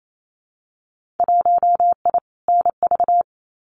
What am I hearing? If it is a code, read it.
Morse code, heard as a keyed tone, about 28 wpm, 700 Hz. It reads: 1I NV
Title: Winter Field Day Exchanges Encore 28wpm